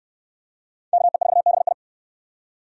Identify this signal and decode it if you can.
Morse code, heard as a keyed tone, about 69 wpm, 700 Hz. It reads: KE4BI